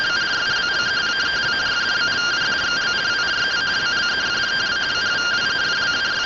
Two-tone teleprinter signal, about 170 Hz shift.
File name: findings / CT SAGITAL 27.gen-27 rtty rx